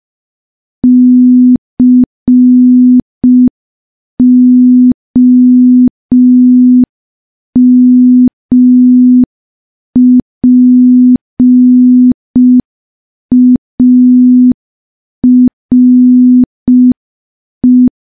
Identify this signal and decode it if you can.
Morse code, heard as a keyed tone, about 5 wpm, 250 Hz. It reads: COMPARE
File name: morse / morse-training-05WPM